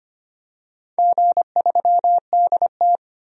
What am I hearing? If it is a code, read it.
Morse code, heard as a keyed tone, about 25 wpm, 700 Hz. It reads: G3DT
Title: DXCC-100 Call Signs - 4 Characters - 25wpm